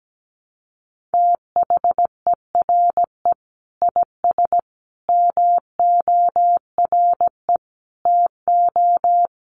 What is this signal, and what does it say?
Morse code, heard as a keyed tone, about 17 wpm, 700 Hz. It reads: THERE IS MORE TO
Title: Sentences from Top 200 Words 17wpm